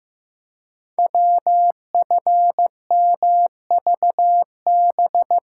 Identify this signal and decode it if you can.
Morse code, heard as a keyed tone, about 15 wpm, 700 Hz. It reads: WFMVB
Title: Code Groups - 5 Letters - 15wpm